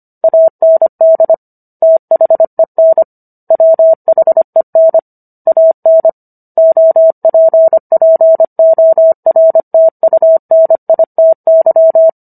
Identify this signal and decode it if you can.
Morse code, heard as a keyed tone, about 25 wpm, 650 Hz. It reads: AND THEN WHEN AN OPPORTUNITY